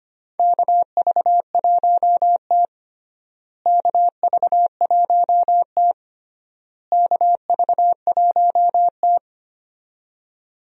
Morse code, heard as a keyed tone, about 25 wpm, 700 Hz. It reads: KV1T KV1T KV1T
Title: USA Call Signs - 4 Characters - Repeated 3x - 25wpm